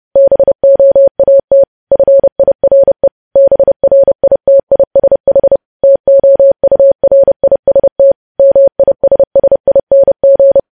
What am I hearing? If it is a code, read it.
Morse code, heard as a keyed tone, about 30 wpm, 550 Hz. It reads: BOAT FIRE BRITISH TOURIST MISSING
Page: news headlines in morse code at 30 wpm